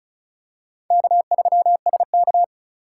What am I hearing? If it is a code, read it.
Morse code, heard as a keyed tone, about 34 wpm, 700 Hz. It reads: K3SK
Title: USA Call Signs - 2x Character Spacing - 35wpm